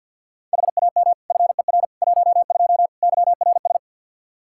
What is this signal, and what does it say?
Morse code, heard as a keyed tone, about 52 wpm, 700 Hz. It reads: HAM FER 12 YRS
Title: Practice QSO - Exchange 3 - More Info - 50wpm